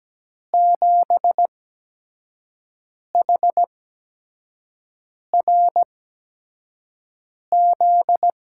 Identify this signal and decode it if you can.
Morse code, heard as a keyed tone, about 17 wpm, spaced as for 5 wpm, 700 Hz. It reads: 7HRZ
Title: USA Call Signs - 5 Characters - 8x Character Spacing - 17wpm